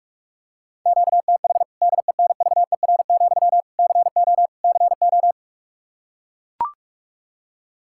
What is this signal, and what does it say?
Morse code, heard as a keyed tone, about 42 wpm, 700 Hz. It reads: QTH DENVER, CO CO
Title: K1USN SST QSO Practice 45wpm